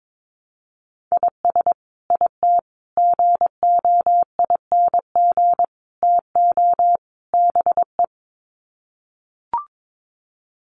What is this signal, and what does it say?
Morse code, heard as a keyed tone, about 22 wpm, 700 Hz. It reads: IS IT GOING TO BE
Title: Sets of 5 Words Encore2 22wpm